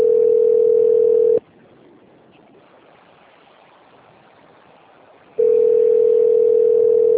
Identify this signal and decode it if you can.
Ringback tone (North America), 440 + 480 Hz.
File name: rg-1600-unknown- - - .40561